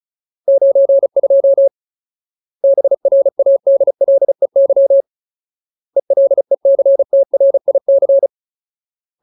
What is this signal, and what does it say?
Morse code, heard as a keyed tone, about 34 wpm, 550 Hz. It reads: 92 BRADLEY ELECTRIC
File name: Ultimate Word List / 35WPM Ultimate